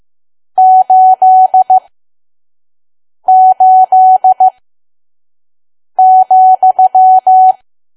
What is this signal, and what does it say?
Morse code, heard as a keyed tone, about 15 wpm, spaced as for 5 wpm, 750 Hz. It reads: 88,
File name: Archive / 5 WPM files / PM